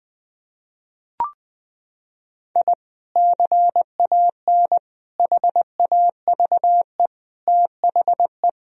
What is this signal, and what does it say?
Morse code, heard as a keyed tone, about 21 wpm, 700 Hz. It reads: I CAN HAVE THE
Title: Sentences from Top 1000 Words - Review Part 2 - 20wpm